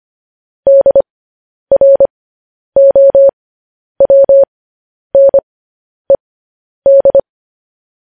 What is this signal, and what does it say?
Morse code, heard as a keyed tone, about 25 wpm, spaced as for 10 wpm, 550 Hz. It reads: DROWNED
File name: podcast / News headlines-1x-25x10